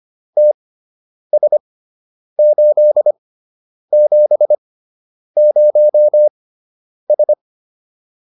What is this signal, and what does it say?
Morse code, heard as a keyed tone, about 25 wpm, spaced as for 9 wpm, 600 Hz. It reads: TS870S